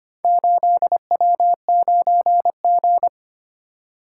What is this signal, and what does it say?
Morse code, heard as a keyed tone, about 25 wpm, 690 Hz. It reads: 8W9G
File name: practice / 25wpm / Four Letter-Number Combinations - 25wpm